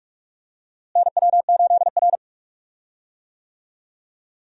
Morse code, heard as a keyed tone, about 41 wpm, 700 Hz. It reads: NW8R
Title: USA Call Signs - 4 Characters - Repeated 2x - 45wpm